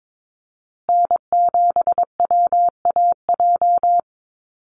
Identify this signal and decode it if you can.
Morse code, heard as a keyed tone, about 22 wpm, 700 Hz. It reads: N7WAJ
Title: USA Call Signs Repeated 2x 22wpm